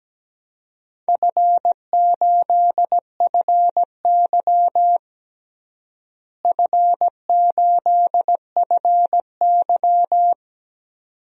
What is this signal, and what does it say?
Morse code, heard as a keyed tone, about 17 wpm, 700 Hz. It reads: F8FY F8FY
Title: DXCC-100 Call Signs - 4 Characters - Repeated 2x - 17wpm